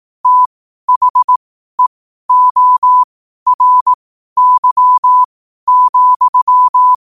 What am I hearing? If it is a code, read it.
Morse code, heard as a keyed tone, about 18 wpm, spaced as for 12 wpm, 1000 Hz. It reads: THEORY,
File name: qotdfiles / 12wpm